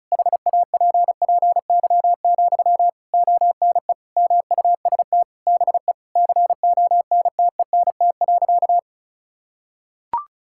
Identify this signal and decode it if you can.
Morse code, heard as a keyed tone, about 36 wpm, 700 Hz. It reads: HAPPY, ONE MUST BE CONTENT.
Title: Sentences from Top 1000 Words - Review Part 6 - 35wpm